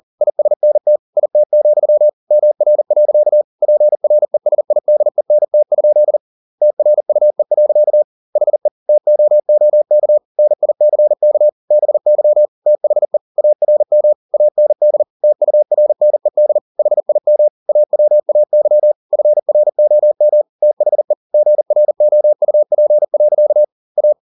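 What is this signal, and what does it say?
Morse code, heard as a keyed tone, about 40 wpm, 600 Hz. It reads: ISNT IT, MR. PRESIDENT? TRUE. HE TOOK NICK BY THE ARM AND TURNED HIM AWAY FROM THE GROUP. A
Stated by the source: Morse code at 40 wpm, practice text, CW med Gnister.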